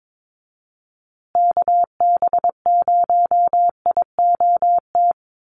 Morse code, heard as a keyed tone, about 22 wpm, 700 Hz. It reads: KB0IOT
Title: USA Call Signs Repeated 2x 22wpm